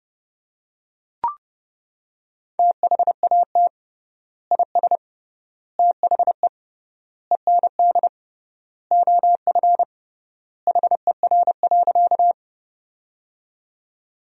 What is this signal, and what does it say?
Morse code, heard as a keyed tone, about 30 wpm, 700 Hz. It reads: THAT IS THE END OF HER.
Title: 3x Word-Spacing - Sentences from Top 200 Words - 30wpm